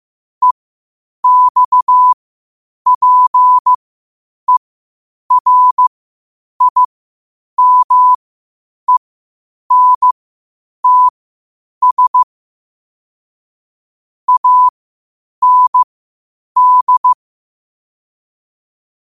Morse code, heard as a keyed tone, about 15 wpm, spaced as for 8 wpm, 1000 Hz. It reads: EXPERIMENTS AND
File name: 8wpm